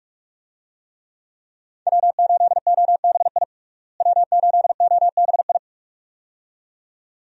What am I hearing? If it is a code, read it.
Morse code, heard as a keyed tone, about 46 wpm, 700 Hz. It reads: W8OBI W8OBI
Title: USA Call Signs - 5 Characters - Repeated 2x - 45wpm